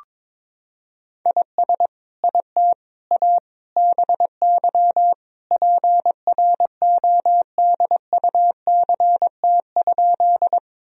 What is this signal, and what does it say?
Morse code, heard as a keyed tone, about 22 wpm, 700 Hz. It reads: IS IT A BY PRODUCT?
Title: Sentences from Top 1000 Words - Review Part 2 - 22wpm